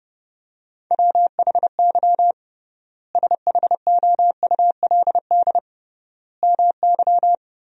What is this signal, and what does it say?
Morse code, heard as a keyed tone, about 30 wpm, 700 Hz. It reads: WHY SHOULD MY
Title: 3x Word-Spacing - Sentences from Top 500 Words - 30wpm